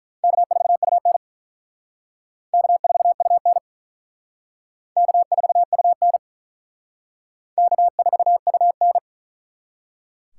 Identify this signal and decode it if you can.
Morse code, heard as a keyed tone, about 42 wpm, 700 Hz. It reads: K4UN K4UN K4UN K4UN
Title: Speed Racing - USA 1x2 Call Signs 35wpm